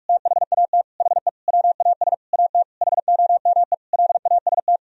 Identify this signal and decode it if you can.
Morse code, heard as a keyed tone, about 46 wpm, 700 Hz. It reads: THAT HE WAS AT HOME LAST
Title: Sentences from Top 1000 Words - Review Part 2 - 45wpm